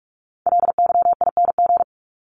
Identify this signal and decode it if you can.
Morse code, heard as a keyed tone, about 44 wpm, 700 Hz. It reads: LYING
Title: ING Suffix - Top 100 Words with Base Words 45wpm